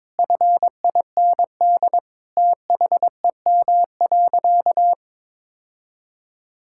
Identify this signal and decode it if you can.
Morse code, heard as a keyed tone, about 22 wpm, 700 Hz. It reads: FIND THEM.
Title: Sentences from Top 1000 Words - Review Part 5 - 22wpm